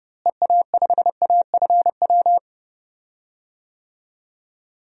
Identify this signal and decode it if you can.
Morse code, heard as a keyed tone, about 32 wpm, 700 Hz. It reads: EA5AFW
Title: DXCC-100 Call Signs - 6 Characters - Repeated 2x - 30wpm